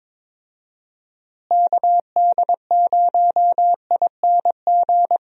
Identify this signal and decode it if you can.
Morse code, heard as a keyed tone, about 22 wpm, 700 Hz. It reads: KD0ING
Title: USA Call Signs - 6 Characters - Repeated 2x - 22wpm